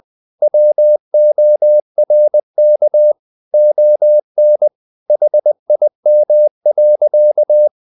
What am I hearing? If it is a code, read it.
Morse code, heard as a keyed tone, about 20 wpm, 600 Hz. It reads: WORK ON HIM.
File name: the old man and the sea part 2 20wpm